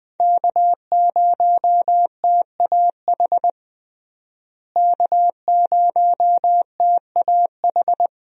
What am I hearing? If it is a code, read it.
Morse code, heard as a keyed tone, about 21 wpm, 700 Hz. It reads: K0TAH K0TAH
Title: USA Call Signs - 5 Characters - Repeated 3x - 20wpm